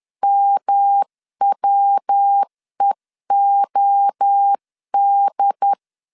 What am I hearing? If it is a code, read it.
Morse code, heard as a keyed tone, about 10 wpm, 800 Hz. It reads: MWEOD
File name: CW / DET / 1 6 1500C S C 10wpm 4252